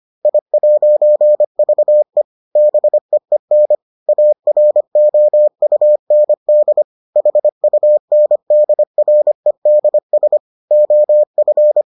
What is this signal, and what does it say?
Morse code, heard as a keyed tone, about 25 wpm, 600 Hz. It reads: I'VE BEEN AROUND HUNDREDS OF